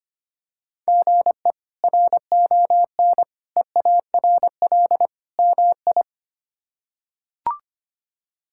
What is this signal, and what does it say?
Morse code, heard as a keyed tone, about 25 wpm, 700 Hz. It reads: GE RON EARL MI
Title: K1USN SST QSO Practice - 25wpm